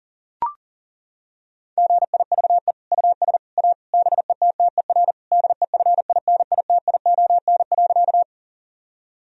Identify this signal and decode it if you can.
Morse code, heard as a keyed tone, about 40 wpm, 690 Hz. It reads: GIVE US A BETTER DEFINITION.
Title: Sentences from Top 1000 Words - Review Part 4 - 40wpm